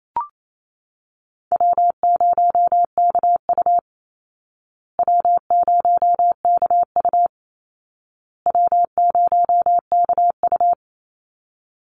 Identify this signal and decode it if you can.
Morse code, heard as a keyed tone, about 28 wpm, 700 Hz. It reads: W0KU W0KU W0KU
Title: Easy Call Signs Repeated 3x - 28wpm